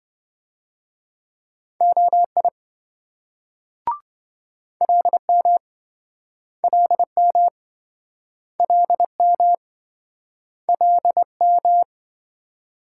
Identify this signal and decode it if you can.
Morse code, heard as a keyed tone, about 28 wpm, 700 Hz. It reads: OI LM LM LM LM